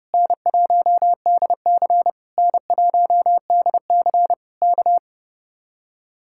Morse code, heard as a keyed tone, about 29 wpm, 700 Hz. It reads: N1DC N1DC K
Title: Speed-Racing Common QSO Phrases - 30wpm